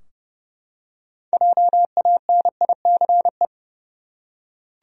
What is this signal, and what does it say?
Morse code, heard as a keyed tone, about 31 wpm, 700 Hz. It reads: JANICE
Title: USA Common Names - 30wpm